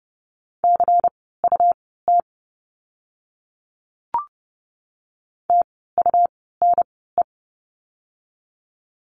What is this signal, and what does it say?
Morse code, heard as a keyed tone, about 30 wpm, spaced as for 17 wpm, 700 Hz. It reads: CUT TUNE
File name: Lesson 17f - Letter u - Words 30wpm